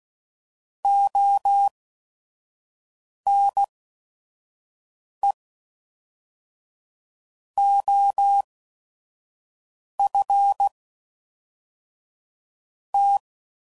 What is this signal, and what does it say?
Morse code, heard as a keyed tone, about 16 wpm, spaced as for 5 wpm, 800 Hz. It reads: ONE OF T
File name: CW Code Groups @5WPM / Technical Subjects / OR Gate